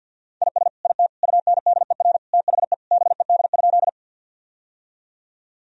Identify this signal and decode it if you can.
Morse code, heard as a keyed tone, about 52 wpm, 700 Hz. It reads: IS IT UNDER THE BED?
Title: Sentences from Top 1000 Words - Review Part 1 - 50wpm